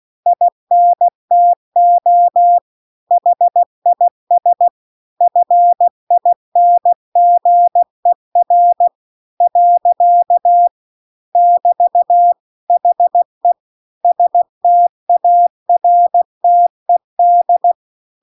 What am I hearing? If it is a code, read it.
Morse code, heard as a keyed tone, about 16 wpm, 700 Hz. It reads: INTO HIS FINGER . <BT> HE STARTED
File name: CW-The-Big-Twoheaded-River-2-16WPM